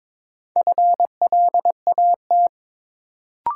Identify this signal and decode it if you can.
Morse code, heard as a keyed tone, about 22 wpm, 700 Hz. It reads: FLAT
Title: Sentences from Top 1000 Words - New Word Drill - 22wpm